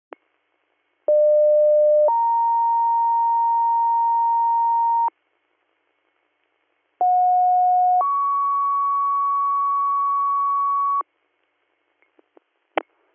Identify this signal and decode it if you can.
Two-tone page, heard as a one-second tone, then a three-second tone, about 600 then 930 Hz, about 730 then 1120 Hz. It is heard from 1.1 to 11.0 s.